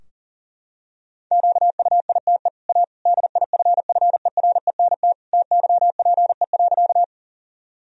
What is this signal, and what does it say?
Morse code, heard as a keyed tone, about 40 wpm, 700 Hz. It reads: QUITE A DIFFERENT TYPE.
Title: Sentences from Top 1000 Words - Review Part 3 - 40wpm